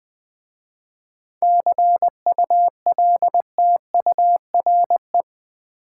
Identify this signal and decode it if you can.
Morse code, heard as a keyed tone, about 20 wpm, 690 Hz. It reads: CULTURE
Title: 7 Letter Words - First 3 Letters Given - 20wpm